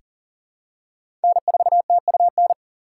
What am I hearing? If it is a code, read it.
Morse code, heard as a keyed tone, about 40 wpm, 700 Hz. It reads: N4TUN
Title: USA Call Signs - 5 Characters - 4x Character Spacing - 40wpm